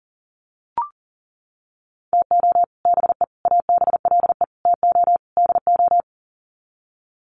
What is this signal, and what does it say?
Morse code, heard as a keyed tone, about 40 wpm, 700 Hz. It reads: TO BE ABLE TO DO
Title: Sets of 5 Words 40wpm